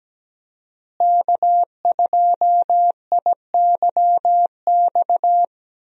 Morse code, heard as a keyed tone, about 17 wpm, 700 Hz. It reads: K2IYX
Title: USA Call Signs - 5 Characters - Repeated 2x - 17wpm